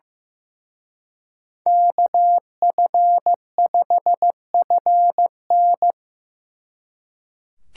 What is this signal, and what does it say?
Morse code, heard as a keyed tone, about 17 wpm, 700 Hz. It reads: KF5FN
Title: Speed Racing - USA 2x2 Call Signs 15wpm